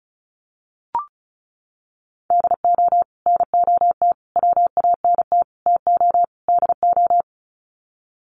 Morse code, heard as a keyed tone, about 34 wpm, 700 Hz. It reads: DO NOT WANT TO DO
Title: Sets of 5 Words 35wpm